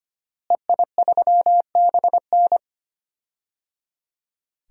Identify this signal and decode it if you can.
Morse code, heard as a keyed tone, about 25 wpm, 700 Hz. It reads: EI3BN
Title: DXCC-100 Call Signs - 5 Characters - Repeated 3x - 25wpm